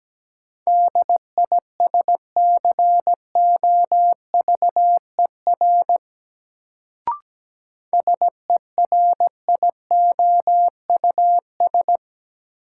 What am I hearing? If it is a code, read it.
Morse code, heard as a keyed tone, about 17 wpm, 700 Hz. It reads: DISCOVER SERIOUS
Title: Sentences from Top 1100 Words - New Word Drill - 17wpm